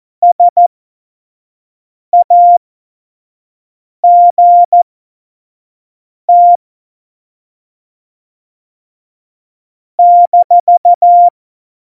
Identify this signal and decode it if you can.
Morse code, heard as a keyed tone, about 14 wpm, spaced as for 5 wpm, 700 Hz. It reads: SAGT -
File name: Morsekurs / Witze / Witz 28 Juli 2020, 25 F70